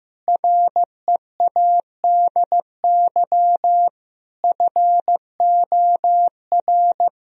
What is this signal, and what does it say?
Morse code, heard as a keyed tone, about 15 wpm, 700 Hz. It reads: READY FOR
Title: Sentences from Top 1000 Words - Review Part 3 - 15wpm